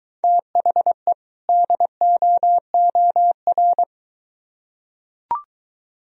Morse code, heard as a keyed tone, about 22 wpm, 700 Hz. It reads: THE DOOR